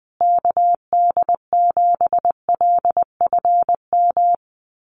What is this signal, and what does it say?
Morse code, heard as a keyed tone, about 20 wpm, 700 Hz. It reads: KD7LFM
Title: USA Call Signs Repeated 2x 20wpm